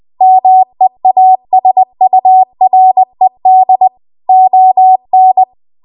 Morse code, heard as a keyed tone, about 20 wpm, 750 Hz. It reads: MEASURED ON
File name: Archive / 20 WPM / 20WPM